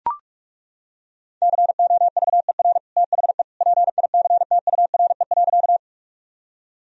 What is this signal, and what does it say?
Morse code, heard as a keyed tone, about 42 wpm, 700 Hz. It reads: COVER THE PICTURE.
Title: Sentences from Top 1000 Words - Review Part 1 - 45wpm